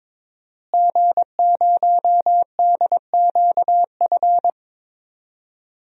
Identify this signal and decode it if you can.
Morse code, heard as a keyed tone, about 22 wpm, 700 Hz. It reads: G0DQF
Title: DXCC-100 Call Signs - 5 Characters - Repeated 3x - 22wpm